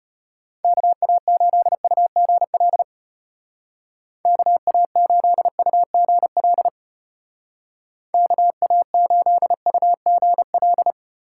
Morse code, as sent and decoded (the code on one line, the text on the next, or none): -.- .- ---.. ..- --. .-.. / -.- .- ---.. ..- --. .-.. / -.- .- ---.. ..- --. .-..
KA8UGL KA8UGL KA8UGL